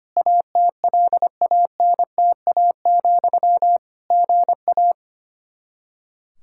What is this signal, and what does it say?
Morse code, heard as a keyed tone, about 25 wpm, 700 Hz. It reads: ATLANTA, GA
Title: Practice QSO - Exchange 1 - Beginning - 25wpm